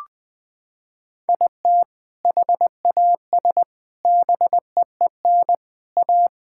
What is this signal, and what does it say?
Morse code, heard as a keyed tone, about 21 wpm, 700 Hz. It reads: IT HAS BEEN A
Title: Sentences with 5-Word Phrases - 20wpm